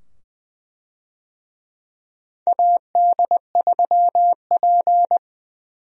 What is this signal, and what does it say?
Morse code, heard as a keyed tone, about 20 wpm, 700 Hz. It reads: AD3P
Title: USA Call Signs - 4 Characters - 20wpm